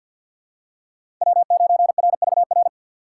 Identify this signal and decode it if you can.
Morse code, heard as a keyed tone, about 47 wpm, 700 Hz. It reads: W9RVR
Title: USA Call Signs - 2x Character Spacing - 50wpm